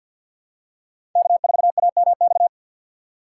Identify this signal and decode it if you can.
Morse code, heard as a keyed tone, about 52 wpm, 700 Hz. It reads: K4AMX
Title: USA Call Signs - 5 Characters - 3x Character Spacing - 50wpm